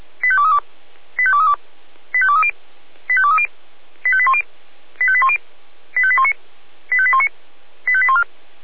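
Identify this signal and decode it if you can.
Five-tone selective call (ZVEI), heard as quick runs of five tones, tones of about 70 ms. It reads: 85232 85232 85230 85230 85810 85810 85819 85819 85814
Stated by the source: ZVEI five-tone selective call received by SDR.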